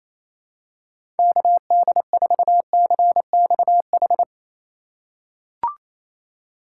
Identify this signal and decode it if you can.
Morse code, heard as a keyed tone, about 28 wpm, 690 Hz. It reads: KD4CXH
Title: USA Call Signs - 6 Characters - 4x Character Spacing - 28wpm